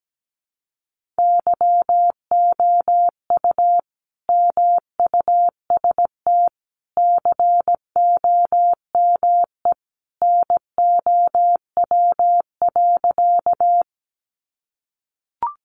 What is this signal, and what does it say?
Morse code, heard as a keyed tone, about 17 wpm, 700 Hz. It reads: YOU MUST COME NOW.
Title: Sentences from Top 200 Words 17wpm